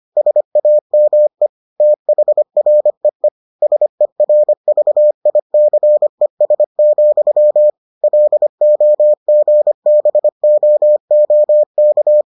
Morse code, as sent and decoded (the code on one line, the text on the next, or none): ... .- -- . / - .... .-. . / ... .-. ...- .. -.-. ... --..-- / .-.. --- --. -... --- --- -.-
SAME THREE SERVICES, LOGBOOK